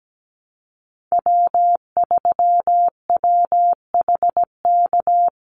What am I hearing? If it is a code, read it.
Morse code, heard as a keyed tone, about 17 wpm, 700 Hz. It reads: W3WHK
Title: Call Signs - 17wpm